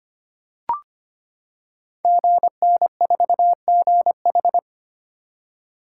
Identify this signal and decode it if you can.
Morse code, heard as a keyed tone, about 25 wpm, 700 Hz. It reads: GN4GH
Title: Code Groups - 5 Letter:Numbers - 25wpm